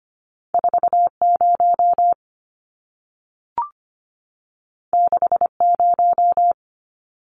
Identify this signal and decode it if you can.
Morse code, heard as a keyed tone, about 25 wpm, 700 Hz. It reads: 40 60
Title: Numbers - 2-Digit 25wpm